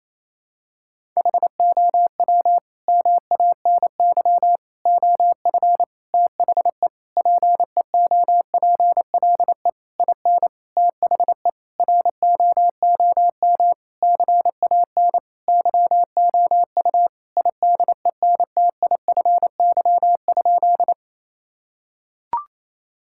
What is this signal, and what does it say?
Morse code, heard as a keyed tone, about 28 wpm, 700 Hz. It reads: HOW MANY OF THE PEOPLE IN THE ROOM CAN YOU IDENTIFY?
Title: Sentences from Top 1000 Words - Review Part 6 - 28wpm